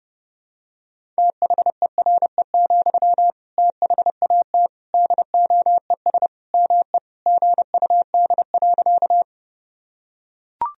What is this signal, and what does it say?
Morse code, heard as a keyed tone, about 30 wpm, 700 Hz. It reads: THERE, THAT DOES ME GUD.
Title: Sentences from Top 200 Words and Abbreviations 30wpm